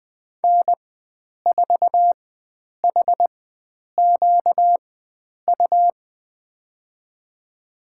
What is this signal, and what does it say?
Morse code, heard as a keyed tone, about 20 wpm, spaced as for 9 wpm, 700 Hz. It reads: N4HQU
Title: USA Call Signs - 5 Characters - 4x Character Spacing - 20wpm